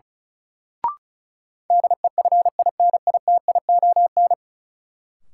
Morse code, heard as a keyed tone, about 36 wpm, 700 Hz. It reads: DEFINITION
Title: Sentences from Top 1000 Words - Review Part 4 - New Word Drill - 35wpm